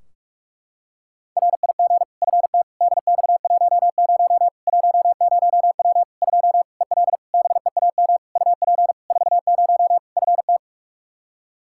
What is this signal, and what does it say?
Morse code, heard as a keyed tone, about 42 wpm, 700 Hz. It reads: RIG FT DX10 10W 2 EL BEAM UP 40 FT
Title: Practice QSO - Exchange 2 - Info - 45wpm